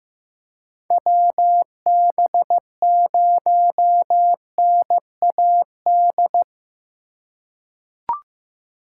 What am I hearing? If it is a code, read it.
Morse code, heard as a keyed tone, about 15 wpm, 700 Hz. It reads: WB0NAD
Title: USA Call Signs - 4x Character Spacing - 15wpm